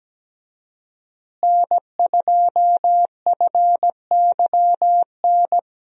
Morse code, heard as a keyed tone, about 17 wpm, 700 Hz. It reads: N2FYN
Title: USA Call Signs - 5 Characters - Repeated 2x - 17wpm